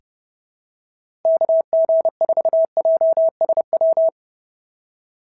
Morse code, heard as a keyed tone, about 26 wpm, 650 Hz. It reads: KG4JSW
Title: SOTA QSO Practice 30wpm